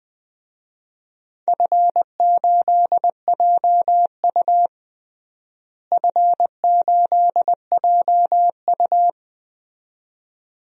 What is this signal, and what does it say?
Morse code, heard as a keyed tone, about 20 wpm, 700 Hz. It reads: F8JU F8JU
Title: DXCC-100 Call Signs - 4 Characters - Repeated 2x - 20wpm